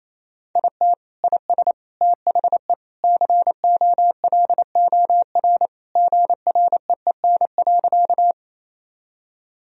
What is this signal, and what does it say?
Morse code, heard as a keyed tone, about 28 wpm, 700 Hz. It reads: IT IS THE COLOR GREEN.
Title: Sentences from Top 1000 Words - Review Part 2 - 28wpm